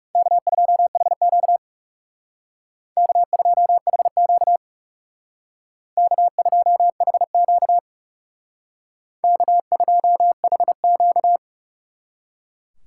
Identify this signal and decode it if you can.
Morse code, heard as a keyed tone, about 36 wpm, 700 Hz. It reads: K2HQ K2HQ K2HQ K2HQ
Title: Speed Racing - USA Call Signs 30wpm